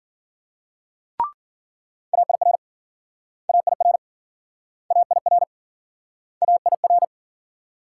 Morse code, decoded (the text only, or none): AIR AIR AIR AIR